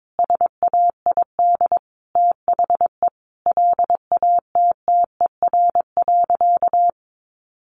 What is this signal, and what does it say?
Morse code, heard as a keyed tone, about 22 wpm, 700 Hz. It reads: SAID THE LATTER.
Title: A Sound Of Thunder 22wpm